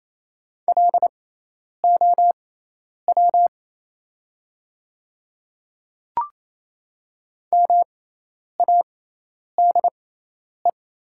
Morse code, heard as a keyed tone, about 28 wpm, spaced as for 10 wpm, 700 Hz. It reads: LOW MADE
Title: 6x Character Spacing - Top 200 Words - 28wpm